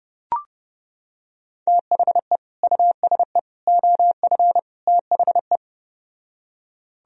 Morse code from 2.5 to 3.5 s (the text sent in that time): USE